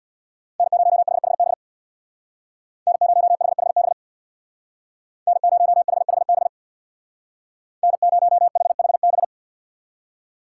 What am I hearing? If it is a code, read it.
Morse code, heard as a keyed tone, about 64 wpm, 700 Hz. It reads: N0HHB N0HHB N0HHB N0HHB